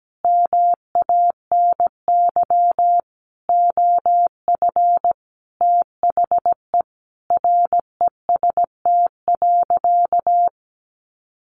Morse code, heard as a keyed tone, about 17 wpm, 700 Hz. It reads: MANY OF THE REST.
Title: Sentences from Top 1500 Words 17wpm